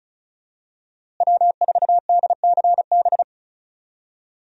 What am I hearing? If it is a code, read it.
Morse code, heard as a keyed tone, about 34 wpm, 700 Hz. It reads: W4DCB
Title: USA Call Signs - 5 Characters - Repeated 3x - 35wpm